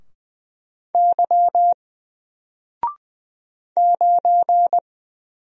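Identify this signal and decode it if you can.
Morse code, heard as a keyed tone, about 20 wpm, 700 Hz. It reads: Y 9
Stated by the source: Single letter and number Morse drill at 20 wpm, from N4LCD.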